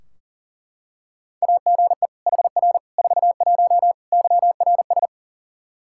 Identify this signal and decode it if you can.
Morse code, heard as a keyed tone, about 40 wpm, 700 Hz. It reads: AGE HR 41 YRS
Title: Practice QSO - Exchange 3 - More Info - 40wpm